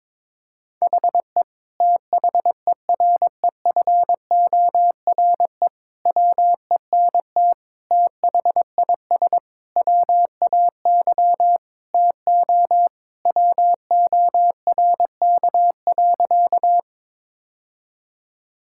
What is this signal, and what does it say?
Morse code, heard as a keyed tone, about 22 wpm, 700 Hz. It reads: HE THEREFORE WENT THIS WAY TO WORK.
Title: Sentences from Top 1000 Words - Review Part 5 - 22wpm